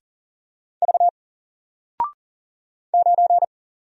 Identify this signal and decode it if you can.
Morse code, heard as a keyed tone, about 39 wpm, 700 Hz. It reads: V 9
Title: Single Letter-Number Alt 40wpm